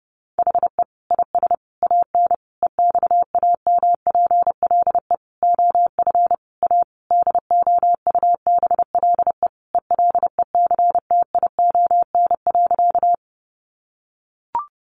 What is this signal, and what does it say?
Morse code, heard as a keyed tone, about 30 wpm, 700 Hz. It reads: HE IS AN EXAMPLE OF A DOUBLE ELECTION.
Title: Sentences from Top 1500 Words 30wpm